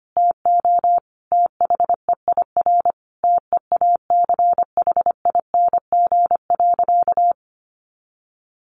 Morse code, heard as a keyed tone, about 25 wpm, 700 Hz. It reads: TO THEIR TEACHING.
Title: Sentences from Top 1100 Words 25wpm